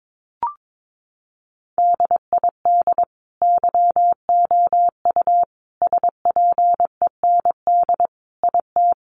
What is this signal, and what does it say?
Morse code, heard as a keyed tone, about 22 wpm, 700 Hz. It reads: DID YOU SPEND IT
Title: Sentences from Top 800 Words 22wpm